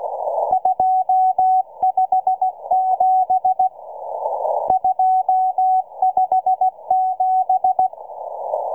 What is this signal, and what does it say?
Morse code, heard as a keyed tone, about 16 wpm, 750 Hz. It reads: 257 257